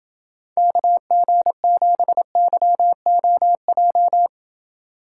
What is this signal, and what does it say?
Morse code, heard as a keyed tone, about 28 wpm, 700 Hz. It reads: KG7YOJ